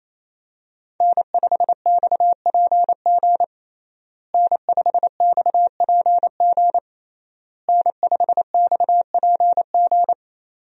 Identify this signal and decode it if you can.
Morse code, heard as a keyed tone, about 28 wpm, 700 Hz. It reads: N5XPG N5XPG N5XPG
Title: USA Call Signs - 5 Characters - Repeated 3x - 28wpm